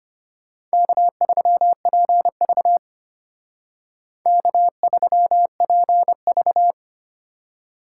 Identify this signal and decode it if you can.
Morse code, heard as a keyed tone, about 32 wpm, 700 Hz. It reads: K3PV K3PV